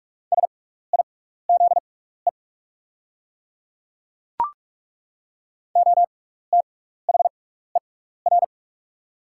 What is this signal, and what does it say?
Morse code, heard as a keyed tone, about 43 wpm, spaced as for 15 wpm, 690 Hz. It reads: SIZE OTHER
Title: 6x Character Spacing - Top 500 Words - 45wpm